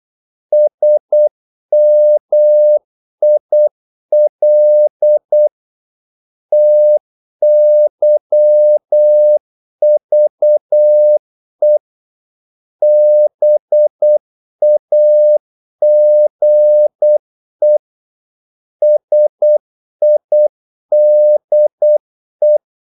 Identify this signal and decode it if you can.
Morse code, heard as a keyed tone, about 8 wpm, 600 Hz. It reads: SMIL TYVE BAGE SIDE
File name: Korte ord DK 8wpm